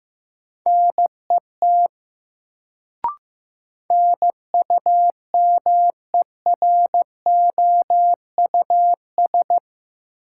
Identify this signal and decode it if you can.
Morse code, heard as a keyed tone, about 15 wpm, 700 Hz. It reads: NET NUMEROUS
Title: Sentences from Top 1400 Words - New Word Drill - 15wpm